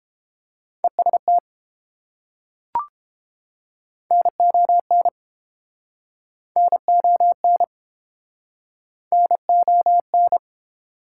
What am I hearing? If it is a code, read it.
Morse code, heard as a keyed tone, about 32 wpm, 700 Hz. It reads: EST NON NON NON